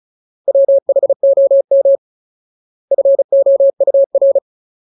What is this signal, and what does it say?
Morse code, heard as a keyed tone, about 34 wpm, 550 Hz. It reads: WHOM FOUR
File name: Ultimate Word List / 35WPM Ultimate